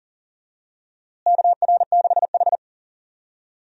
Morse code, heard as a keyed tone, about 40 wpm, 700 Hz. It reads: KR6H
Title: USA Call Signs - 4 Characters - Repeated 2x - 40wpm